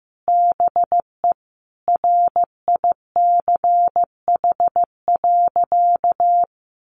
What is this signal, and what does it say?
Morse code, heard as a keyed tone, about 14 wpm, 700 Hz. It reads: BE RICH.
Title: Sentences from Top 1100 Words 15wpm